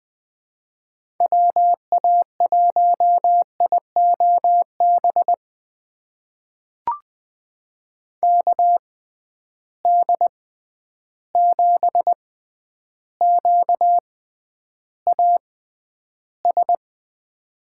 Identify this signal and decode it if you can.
Morse code, heard as a keyed tone, about 20 wpm, spaced as for 10 wpm, 700 Hz. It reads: WA1IOB KD7QAS
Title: USA Call Signs - 6 Characters - 6x Character Spacing - 20wpm